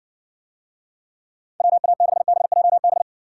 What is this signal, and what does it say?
Morse code, heard as a keyed tone, about 60 wpm, 700 Hz. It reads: WA6BJB